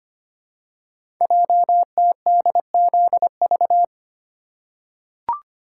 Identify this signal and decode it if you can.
Morse code, heard as a keyed tone, about 25 wpm, 700 Hz. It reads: JTDZV
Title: Code Groups - 5 Letters - 25wpm